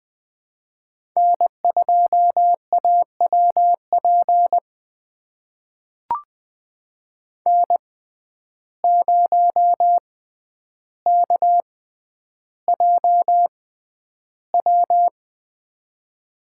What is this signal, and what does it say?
Morse code, heard as a keyed tone, about 20 wpm, spaced as for 10 wpm, 700 Hz. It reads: N2AWP N0KJW
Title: USA Call Signs - 5 Characters - 6x Character Spacing - 20wpm